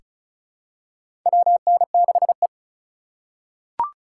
Morse code, heard as a keyed tone, about 36 wpm, 700 Hz. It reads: WN6E
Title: USA Call Signs - 4 Characters - 2x Character Spacing - 35wpm